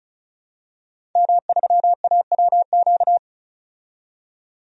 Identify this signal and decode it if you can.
Morse code, heard as a keyed tone, about 34 wpm, 700 Hz. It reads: M3AWQ
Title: DXCC-100 Call Signs - 5 Characters - 35wpm